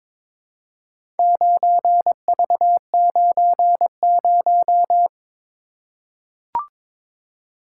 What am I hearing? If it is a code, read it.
Morse code, heard as a keyed tone, about 22 wpm, 700 Hz. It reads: 9V90
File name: practice / 22wpm / Four Letter-Number Combinations - 22wpm